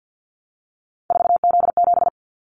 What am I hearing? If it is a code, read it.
Morse code, heard as a keyed tone, about 50 wpm, 700 Hz. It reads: 4Z7
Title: Callsign Prefixes for Top 50 DXCC - 50wpm